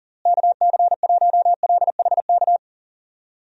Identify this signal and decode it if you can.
Morse code, heard as a keyed tone, about 40 wpm, 700 Hz. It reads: KC1LHK
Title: USA Call Signs - 6 Characters - Repeated 3x - 40wpm